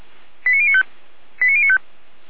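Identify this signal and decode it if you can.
Five-tone selective call (ZVEI), heard as quick runs of five tones, tones of about 70 ms. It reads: 89095 89095
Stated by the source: ZVEI five-tone call received by SDR, from audio.f85360.feuerwehr-orlamuende.org (Feuerwehr Orlamünde).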